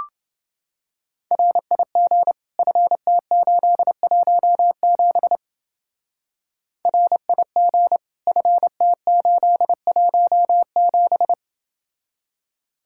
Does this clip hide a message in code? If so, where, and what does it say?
Yes: Morse code from 1.3 to 11.3 s, heard as a keyed tone, about 29 wpm, 700 Hz. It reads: RIG FT817 RIG FT817